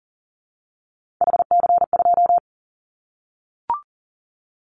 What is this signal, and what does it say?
Morse code, heard as a keyed tone, about 40 wpm, 700 Hz. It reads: HC2
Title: Callsign Prefixes for Top 100 DXCC - 40wpm